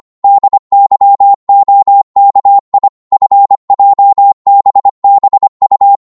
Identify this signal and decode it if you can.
Morse code, heard as a keyed tone, about 25 wpm, 800 Hz. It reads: DYOKI FJBBU